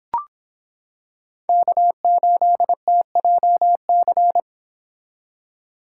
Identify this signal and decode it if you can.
Morse code, heard as a keyed tone, about 26 wpm, 700 Hz. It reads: K8TJC